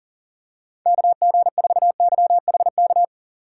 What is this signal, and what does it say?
Morse code, heard as a keyed tone, about 40 wpm, 700 Hz. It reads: KG4YHK
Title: USA Call Signs - 2x Character Spacing - 40wpm